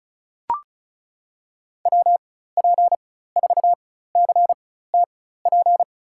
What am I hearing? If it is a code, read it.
Morse code, heard as a keyed tone, about 36 wpm, spaced as for 16 wpm, 700 Hz. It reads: WP4CTP
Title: USA Call Signs - 6 Characters - 4x Character Spacing - 35wpm